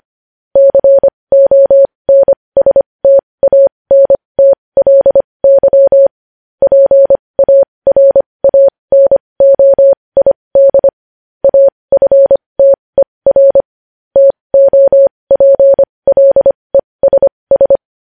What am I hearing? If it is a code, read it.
Morse code, heard as a keyed tone, about 25 wpm, spaced as for 20 wpm, 550 Hz. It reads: CONSTANTLY PARANOID AFTER TOPLESS
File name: podcast / News headlines-1x-25x20